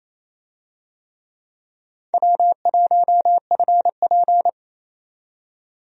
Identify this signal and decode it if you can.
Morse code, heard as a keyed tone, about 28 wpm, 700 Hz. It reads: W1FP
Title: USA Call Signs - 4 Characters - 28wpm